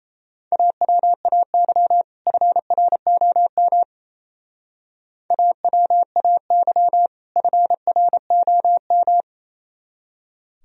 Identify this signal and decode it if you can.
Morse code, heard as a keyed tone, about 34 wpm, 700 Hz. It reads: AWAY FROM AWAY FROM
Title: Speed-Racing - Sets of 2 Words 28wpm